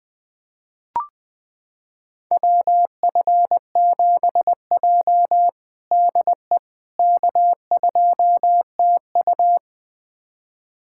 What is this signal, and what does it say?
Morse code, heard as a keyed tone, about 21 wpm, 700 Hz. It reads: WF7J DE K2TU
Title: USA Callsign DE Callsign - 4 Characters - 20wpm